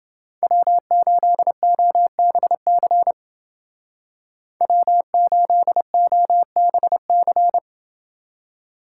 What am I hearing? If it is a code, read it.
Morse code, heard as a keyed tone, about 28 wpm, 700 Hz. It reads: W8OBC W8OBC